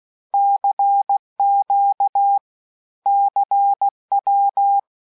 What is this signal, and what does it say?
Morse code, heard as a keyed tone, about 16 wpm, 800 Hz. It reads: CQ CW
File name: cours-du-lundi files / cours-1580-abreviations-mpm-15